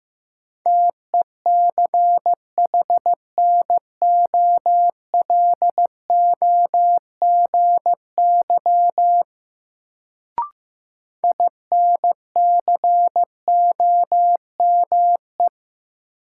Morse code, heard as a keyed tone, about 15 wpm, 700 Hz. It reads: TECHNOLOGY INCOME
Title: Sentences from Top 1000 Words - Review Part 5 - New Word Drill - 15wpm